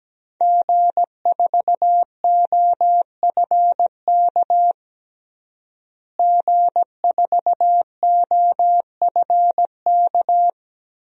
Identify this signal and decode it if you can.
Morse code, heard as a keyed tone, about 17 wpm, 700 Hz. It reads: G4OFK G4OFK
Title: DXCC-100 Call Signs - 5 Characters - Repeated 3x - 17wpm